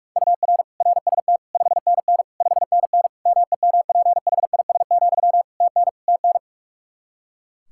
Morse code, heard as a keyed tone, about 46 wpm, 700 Hz. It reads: UR RST 5NN 5NN MEMPHIS, TN TN
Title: Practice QSO - Exchange 1 - Beginning - 45wpm